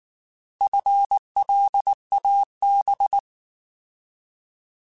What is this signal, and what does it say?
Morse code, heard as a keyed tone, about 19 wpm, 800 Hz. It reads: FLAB
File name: Alphabetical / Alphabetical mp3s / Fs @18